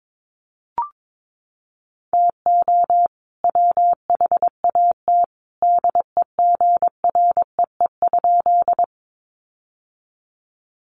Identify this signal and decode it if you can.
Morse code, heard as a keyed tone, about 22 wpm, 700 Hz. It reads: TO WHAT DEGREE?
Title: Sentences from Top 1500 Words 22wpm